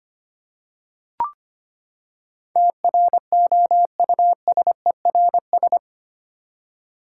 Morse code, heard as a keyed tone, about 25 wpm, 700 Hz. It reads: TROUSERS
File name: Eight Letter Words - 25wpm